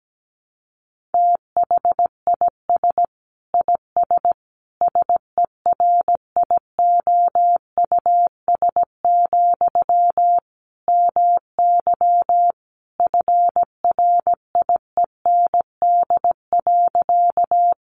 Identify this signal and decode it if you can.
Morse code, heard as a keyed tone, about 17 wpm, 700 Hz. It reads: THIS IS SERIOUS, MY FRIEND.
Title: Sentences from Top 1100 Words 17wpm